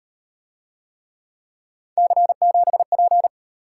Morse code, heard as a keyed tone, about 38 wpm, 700 Hz. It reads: C7P